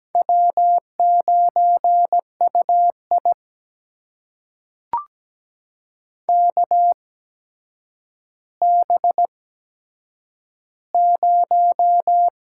Morse code, heard as a keyed tone, about 17 wpm, spaced as for 8 wpm, 700 Hz. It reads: W9UI KB0
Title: USA Call Signs - 8x Character Spacing - 17wpm